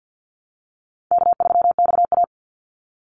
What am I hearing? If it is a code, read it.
Morse code, heard as a keyed tone, about 50 wpm, 700 Hz. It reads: K3XA
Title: Easy Call Signs - 50wpm